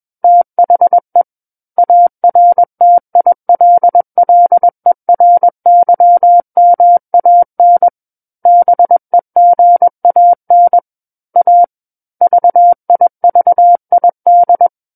Morse code, heard as a keyed tone, about 21 wpm, 700 Hz. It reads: THE ARTILLERYMAN BEGAN A VIVID
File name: Book One / 12-Chapter 12 - 21 WPM